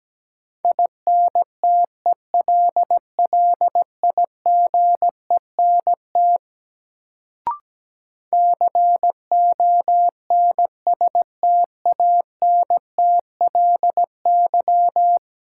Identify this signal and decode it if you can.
Morse code, heard as a keyed tone, about 17 wpm, 700 Hz. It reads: INTELLIGENT CONSTANTLY
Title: Sentences from Top 1000 Words - Review Part 7 - New Word Drill - 17wpm